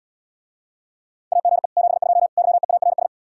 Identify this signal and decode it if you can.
Morse code, heard as a keyed tone, about 79 wpm, 700 Hz. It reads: AGE 72 YEARS